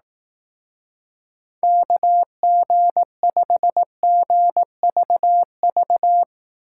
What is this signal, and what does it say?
Morse code, heard as a keyed tone, about 19 wpm, 700 Hz. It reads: KG5GVV